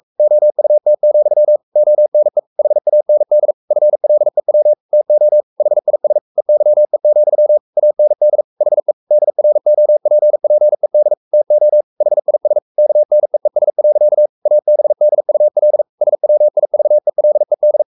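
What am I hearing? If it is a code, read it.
Morse code, heard as a keyed tone, about 44 wpm, 600 Hz. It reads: OUT, ONE HAND FLEW TO HIS EYE, AND HE DROPPED TO HIS KNEES. ABDUL SWIVELED
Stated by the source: Morse code at 43 wpm sending Ken Follett's novel Never in English, from CW med Gnister.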